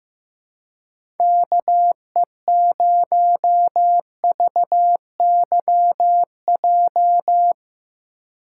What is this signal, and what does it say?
Morse code, heard as a keyed tone, about 15 wpm, 700 Hz. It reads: KE0VYJ
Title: USA Call Signs - 6 Characters - 2x Character Spacing - 15wpm